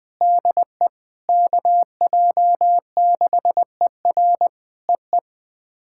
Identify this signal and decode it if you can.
Morse code, heard as a keyed tone, about 20 wpm, 700 Hz. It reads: DE KJ6ER EE
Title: POTA QSO Practice - 20wpm